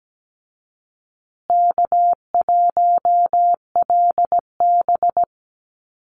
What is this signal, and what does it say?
Morse code, heard as a keyed tone, about 17 wpm, 700 Hz. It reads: K1LB
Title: Easy Call Signs Repeated 3x - 17wpm